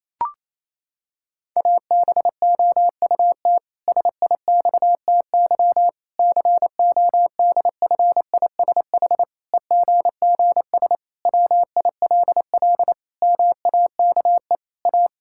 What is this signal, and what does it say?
Morse code, heard as a keyed tone, about 28 wpm, 700 Hz. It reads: ABOUT SIXTY CODFISH EGGS WILL MAKE A
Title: Pangram Challenge v2 - 28wpm